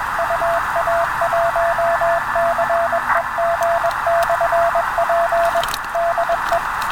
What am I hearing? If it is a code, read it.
Morse code, heard as a keyed tone, about 21 wpm, 680 Hz, with radio noise. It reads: UA1CEG/P DE